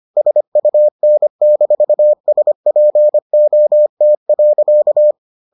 Morse code, heard as a keyed tone, about 25 wpm, 600 Hz. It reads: SUN-SPOT.